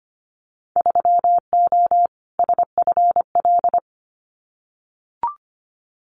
Morse code, heard as a keyed tone, about 25 wpm, 700 Hz. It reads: 3O SFL
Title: Winter Field Day Exchanges 25wpm